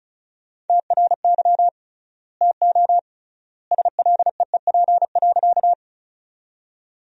Morse code, heard as a keyed tone, about 36 wpm, 700 Hz. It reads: TRY TO SLEEP.
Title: 3x Word-Spacing - Sentences from Top 500 Words - 35wpm